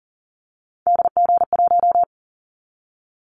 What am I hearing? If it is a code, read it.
Morse code, heard as a keyed tone, about 40 wpm, 700 Hz. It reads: DG1
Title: Callsign Prefixes for Top 25 DXCC - 40wpm